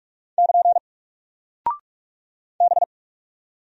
Morse code, heard as a keyed tone, about 44 wpm, 700 Hz. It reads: <KN> B
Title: Single Character (Letter:Number:Punctuation:Prosign) Alt 45wpm